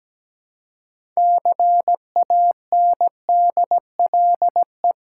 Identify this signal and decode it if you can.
Morse code, heard as a keyed tone, about 17 wpm, 700 Hz. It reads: CANDLE
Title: Six Letter Words - 17wpm